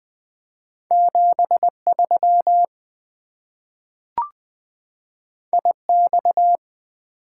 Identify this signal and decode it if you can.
Morse code, heard as a keyed tone, about 20 wpm, 700 Hz. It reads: 7 3 I X
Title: Two Letter-Number Combinations - 20wpm